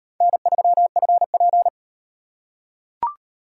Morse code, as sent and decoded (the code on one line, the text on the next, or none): -. ...-- ..-. .--.
N3FP